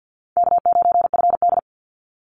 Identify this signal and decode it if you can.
Morse code, heard as a keyed tone, about 50 wpm, 700 Hz. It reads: K9FD
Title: Easy Call Signs Repeated 3x - 50wpm